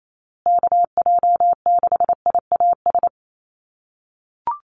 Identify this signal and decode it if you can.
Morse code, heard as a keyed tone, about 28 wpm, 700 Hz. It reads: KJ6IAS
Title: USA Call Signs Repeated 2x 28wpm